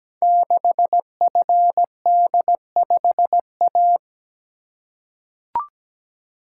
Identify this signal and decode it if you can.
Morse code, heard as a keyed tone, about 17 wpm, 700 Hz. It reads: 6FD5A
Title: Code Groups - 5 Letter:Numbers - 17wpm